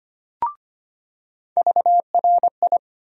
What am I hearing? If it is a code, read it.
Morse code, heard as a keyed tone, about 25 wpm, 700 Hz. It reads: VRI
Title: USA Call Sign Postfixes - 3 Letters 25wpm